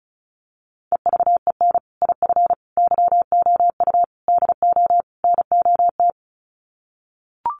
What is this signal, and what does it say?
Morse code, heard as a keyed tone, about 34 wpm, 700 Hz. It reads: EVEN IF YOU DO NOT
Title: Sets of 5 Words Encore 35wpm